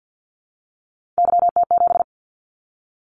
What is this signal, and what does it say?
Morse code, heard as a keyed tone, about 50 wpm, 700 Hz. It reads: YT7
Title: Callsign Prefixes for Top 25 DXCC - 50wpm